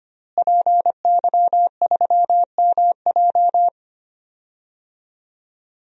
Morse code, heard as a keyed tone, about 25 wpm, 700 Hz. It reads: PY3MJ
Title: DXCC-100 Call Signs - 5 Characters - Repeated 2x - 25wpm